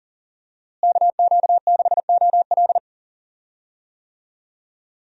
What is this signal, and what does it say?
Morse code, heard as a keyed tone, about 40 wpm, 700 Hz. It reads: KQ6OL
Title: USA Call Signs - 5 Characters - Repeated 3x - 40wpm